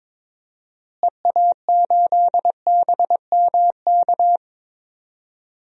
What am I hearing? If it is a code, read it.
Morse code, heard as a keyed tone, about 22 wpm, 700 Hz. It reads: EA8BMK
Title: DXCC-100 Call Signs - 6 Characters - Repeated 2x - 22wpm